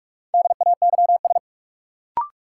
Morse code, heard as a keyed tone, about 45 wpm, 700 Hz. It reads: DAYS
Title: Sets of 5 Words Encore - New Word Drill - 45wpm